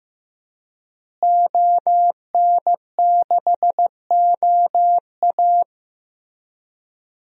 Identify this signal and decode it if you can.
Morse code, heard as a keyed tone, about 16 wpm, 700 Hz. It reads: ON6OA
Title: DXCC-100 Call Signs - 5 Characters - Repeated 3x - 15wpm